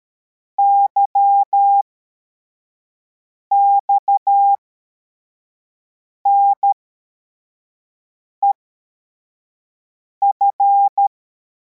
Morse code, heard as a keyed tone, about 13 wpm, spaced as for 4 wpm, 800 Hz. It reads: YXNEF